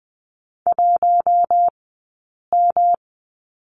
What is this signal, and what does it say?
Morse code, heard as a keyed tone, about 20 wpm, spaced as for 8 wpm, 700 Hz. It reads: 1M
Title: 217 Lesson 35d - Number 0 - Two Characters 20wpm